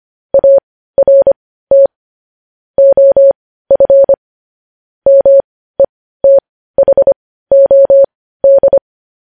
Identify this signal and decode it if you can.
Morse code, heard as a keyed tone, about 25 wpm, spaced as for 15 wpm, 550 Hz. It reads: ART OF METHOD